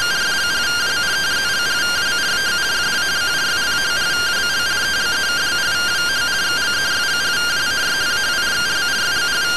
Two-tone teleprinter signal, about 170 Hz shift.